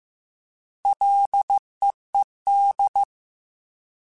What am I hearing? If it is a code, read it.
Morse code, heard as a keyed tone, about 15 wpm, 800 Hz. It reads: LEED
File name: Ls / Single L m3s / Single L Prefixe 3x2 Twice @13